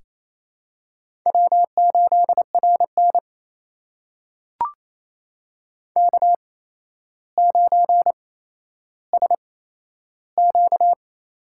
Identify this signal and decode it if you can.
Morse code, heard as a keyed tone, about 28 wpm, spaced as for 12 wpm, 700 Hz. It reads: W8RN K9SQ
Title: USA Call Signs - 4 Characters - 8x Character Spacing - 28wpm